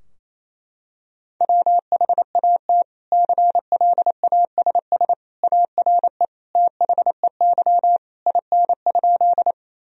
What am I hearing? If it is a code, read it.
Morse code, heard as a keyed tone, about 28 wpm, 700 Hz. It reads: WHAT CLASS ARE THEY IN?
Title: Sentences from Top 1000 Words - Review Part 2 - 28wpm